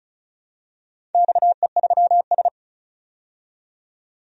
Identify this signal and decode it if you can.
Morse code, heard as a keyed tone, about 36 wpm, 700 Hz. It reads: XE3S
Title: DXCC-100 Call Signs - 4 Characters - 35wpm